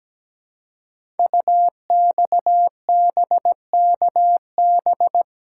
Morse code, heard as a keyed tone, about 17 wpm, 700 Hz. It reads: UXBKB
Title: Code Groups - 5 Letters - 17wpm